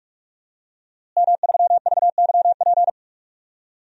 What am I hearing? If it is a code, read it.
Morse code, heard as a keyed tone, about 41 wpm, 700 Hz. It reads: M3VYP
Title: DXCC-100 Call Signs - 5 Characters - Repeated 3x - 45wpm